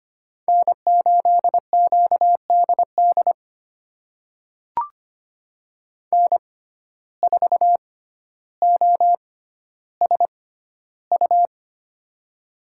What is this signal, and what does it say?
Morse code, heard as a keyed tone, about 25 wpm, spaced as for 13 wpm, 700 Hz. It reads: N8QDD N4OSU
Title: USA Call Signs - 5 Characters - 6x Character Spacing - 25wpm